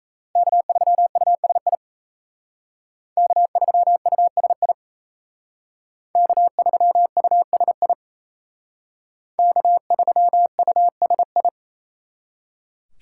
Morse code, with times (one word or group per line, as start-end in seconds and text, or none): 0.4-1.8 K3USI
3.2-4.7 K3USI
6.2-7.9 K3USI
9.4-11.5 K3USI